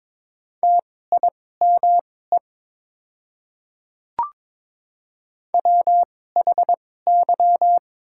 Morse code, heard as a keyed tone, about 21 wpm, spaced as for 16 wpm, 690 Hz. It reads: TIME WHY
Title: 2x Character Spacing - Top 300 Words - 22wpm